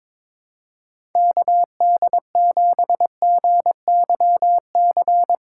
Morse code, heard as a keyed tone, about 22 wpm, 700 Hz. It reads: KD7GYC
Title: USA Callsign DE Callsign - 6 Characters - 22wpm